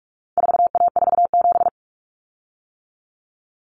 Morse code, heard as a keyed tone, about 44 wpm, 700 Hz. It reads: 4A47
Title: Common Cut Numbers - 4-Digit 45wpm